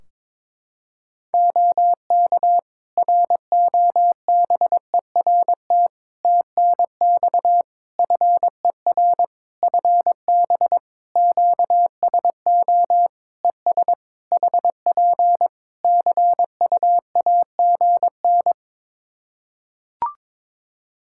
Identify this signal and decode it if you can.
Morse code, heard as a keyed tone, about 22 wpm, 700 Hz. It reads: OK ROBERT TNX FER FB QSO ES HP CUAGN
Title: Practice QSO - Exchange 4 - Ending - 22wpm